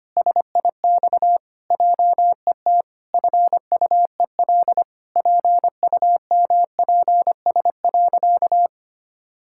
Morse code, heard as a keyed tone, about 25 wpm, 700 Hz. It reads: SIX JET FUEL PUMPS.
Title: Pangram Challenge v2 - 25wpm